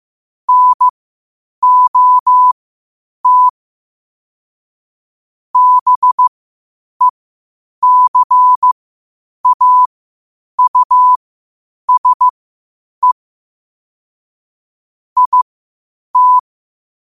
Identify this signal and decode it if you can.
Morse code, heard as a keyed tone, about 15 wpm, spaced as for 8 wpm, 1000 Hz. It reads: NOT BECAUSE IT
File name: qotdfiles / 8wpm